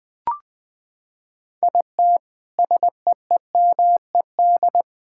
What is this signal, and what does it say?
Morse code, heard as a keyed tone, about 20 wpm, 700 Hz. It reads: IT SEEMED
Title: Sentences with 8-Word Phrases - 20wpm